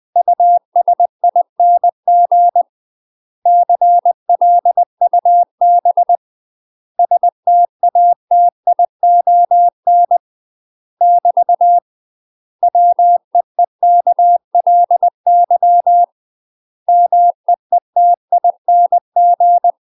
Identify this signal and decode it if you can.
Morse code, heard as a keyed tone, about 20 wpm, 700 Hz. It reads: USING CLUB STATION <BT> WEEKLY MEETING
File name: UK MASTER QSOs TRAINING / 20WPM / UK MASTER QSOs 0013